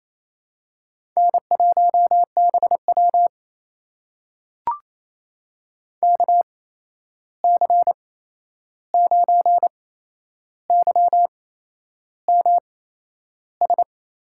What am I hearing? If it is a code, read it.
Morse code, heard as a keyed tone, about 28 wpm, spaced as for 11 wpm, 700 Hz. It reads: N1BW KC9YMS
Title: USA Call Signs - 8x Character Spacing - 28wpm